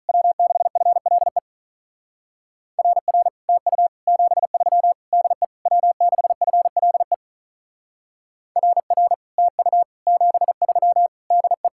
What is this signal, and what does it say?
Morse code, heard as a keyed tone, about 44 wpm, 700 Hz. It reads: W6FLE RR TU 73 DE W6FLE RR TU 73 DE